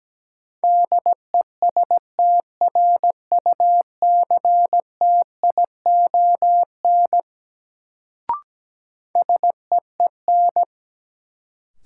Morse code, heard as a keyed tone, about 17 wpm, 700 Hz. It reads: DESTRUCTION SEEN
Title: Sets of 5 Words Encore - New Word Drill - 17wpm